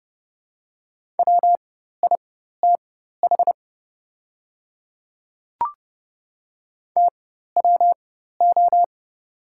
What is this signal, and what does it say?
Morse code, heard as a keyed tone, about 30 wpm, spaced as for 14 wpm, 690 Hz. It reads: WITH TWO
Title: 4x Character Spacing - Top 100 Words - 30wpm